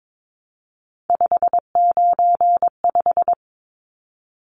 Morse code, heard as a keyed tone, about 22 wpm, 700 Hz. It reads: 595
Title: Numbers - 3-Digit 22wpm